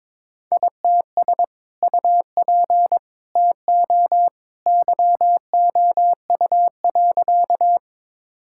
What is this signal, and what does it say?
Morse code, heard as a keyed tone, about 22 wpm, 700 Hz. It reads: ITS UP TO YOU.
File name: practice / 22wpm / Sentences with 2-Word Phrases - 22wpm